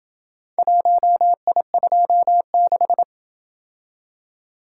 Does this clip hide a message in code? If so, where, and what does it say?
Yes: Morse code from 0.6 to 3.0 s, heard as a keyed tone, about 27 wpm, 700 Hz. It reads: 1I26